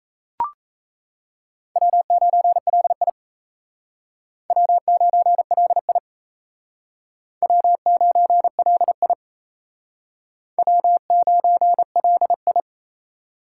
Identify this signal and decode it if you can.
Morse code, heard as a keyed tone, about 36 wpm, 700 Hz. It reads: W9LI W9LI W9LI W9LI